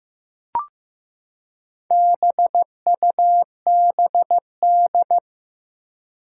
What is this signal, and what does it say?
Morse code, heard as a keyed tone, about 15 wpm, 700 Hz. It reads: BUBD
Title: Four Challenging Letter-Number Combinations 15wpm